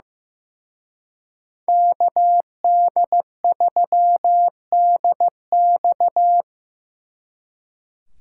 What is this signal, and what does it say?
Morse code, heard as a keyed tone, about 18 wpm, 700 Hz. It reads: KD3DX
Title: Speed Racing - USA 2x2 Call Signs 15wpm